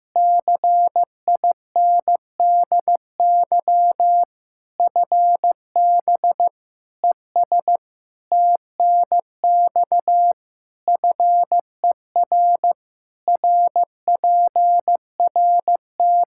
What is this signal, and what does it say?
Morse code, heard as a keyed tone, about 16 wpm, 700 Hz. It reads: CINDY FB ES TNX FER RPRT
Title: K1USN SST QSO Practice 15wpm